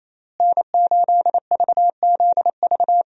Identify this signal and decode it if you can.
Morse code, heard as a keyed tone, about 28 wpm, 700 Hz. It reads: N8VZV
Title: USA Call Signs - 5 Characters - Repeated 3x - 28wpm